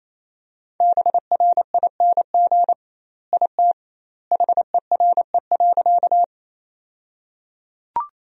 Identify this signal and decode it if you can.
Morse code, heard as a keyed tone, about 28 wpm, 700 Hz. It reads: BRING IT HERE.
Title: 2x Word-Spacing - Sentences from Top 500 Words - 28wpm